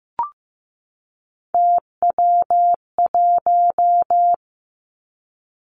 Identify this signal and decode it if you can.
Morse code, heard as a keyed tone, about 15 wpm, 700 Hz. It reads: TW1
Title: Three Challenging Letter-Number Combinations 15wpm